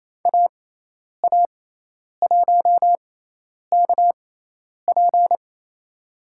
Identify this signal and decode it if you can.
Morse code, heard as a keyed tone, about 28 wpm, spaced as for 10 wpm, 700 Hz. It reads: AA1KP
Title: USA Call Signs - 6x Character Spacing - 28wpm